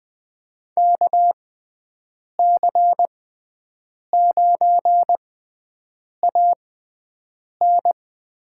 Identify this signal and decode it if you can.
Morse code, heard as a keyed tone, about 20 wpm, spaced as for 7 wpm, 700 Hz. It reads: KC9AN
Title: USA Call Signs - 6 Characters - 6x Character Spacing - 20wpm